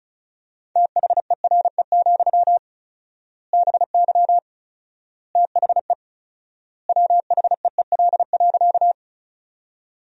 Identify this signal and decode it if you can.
Morse code, heard as a keyed tone, about 36 wpm, 700 Hz. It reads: THERE, BY THE WHEEL.
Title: 4x Word-Spacing - Sentences from Top 500 Words - 35wpm